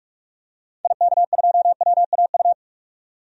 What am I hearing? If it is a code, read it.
Morse code, heard as a keyed tone, about 46 wpm, 700 Hz. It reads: IK2WAU
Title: DXCC-100 Call Signs - 6 Characters - 45wpm